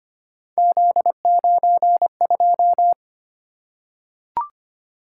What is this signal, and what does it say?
Morse code, heard as a keyed tone, about 25 wpm, 700 Hz. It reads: Z92
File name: Three Letter-Number Combinations - 25wpm